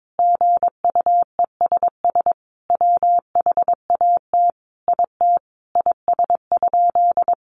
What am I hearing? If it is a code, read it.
Morse code, heard as a keyed tone, about 22 wpm, 700 Hz. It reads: GUESS WHAT IT IS?
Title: Sentences from Top 1500 Words 22wpm